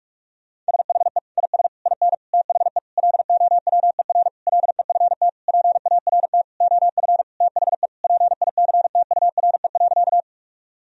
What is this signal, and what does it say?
Morse code, heard as a keyed tone, about 42 wpm, 700 Hz. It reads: SHE IS IN THE LOWER LEFT PART OF THE PICTURE.
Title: Sentences from Top 1000 Words - Review Part 4 - 45wpm